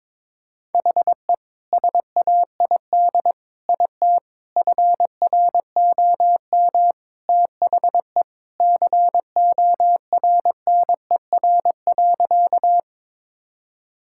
Morse code, heard as a keyed tone, about 21 wpm, 700 Hz. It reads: HE SAID IT FROM THE CORNER.
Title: Sentences from Top 1600 Words 22wpm